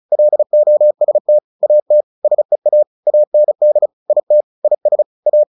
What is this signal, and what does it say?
Morse code, heard as a keyed tone, about 34 wpm, 600 Hz. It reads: LOST AT SEA AND IT IS A
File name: the old man and the sea 2 35wpm